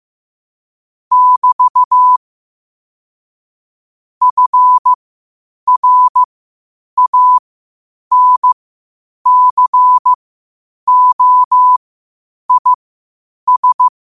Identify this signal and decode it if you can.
Morse code, heard as a keyed tone, about 15 wpm, spaced as for 8 wpm, 1000 Hz. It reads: <BT> FRANCOIS
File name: qotdfiles / 8wpm